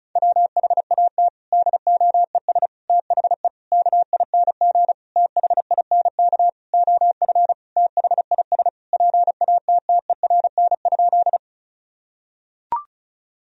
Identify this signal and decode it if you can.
Morse code, heard as a keyed tone, about 34 wpm, 700 Hz. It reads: WHAT DOES THE KING THINK OF THIS PATTERN?
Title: Sentences from Top 1000 Words - Review Part 2 - 35wpm